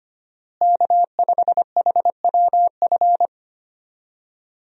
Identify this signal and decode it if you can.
Morse code, heard as a keyed tone, about 25 wpm, 700 Hz. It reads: K5HWF
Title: USA Call Signs - 5 Characters - Repeated 3x - 25wpm